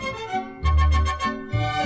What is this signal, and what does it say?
There is no code.